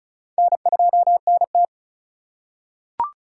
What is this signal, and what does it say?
Morse code, heard as a keyed tone, about 36 wpm, 700 Hz. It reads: N2NT
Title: USA Call Signs - 8x Character Spacing - 35wpm